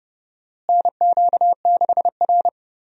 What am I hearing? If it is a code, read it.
Morse code, heard as a keyed tone, about 30 wpm, 700 Hz. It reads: NQ6R
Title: USA Call Signs - 4 Characters - Repeated 2x - 30wpm